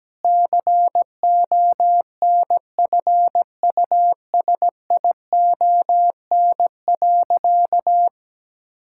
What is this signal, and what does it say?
Morse code, heard as a keyed tone, about 17 wpm, 700 Hz. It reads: CONFUSION.
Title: Sentences from Top 1600 Words 17wpm